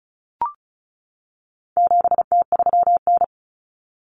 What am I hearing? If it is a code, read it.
Morse code, heard as a keyed tone, about 34 wpm, 700 Hz. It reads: 7T3N
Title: Common Cut Numbers - 4-Digit 35wpm